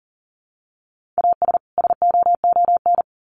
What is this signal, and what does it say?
Morse code, heard as a keyed tone, about 40 wpm, 700 Hz. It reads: AS SOON
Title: Sets of 3 Words 40wpm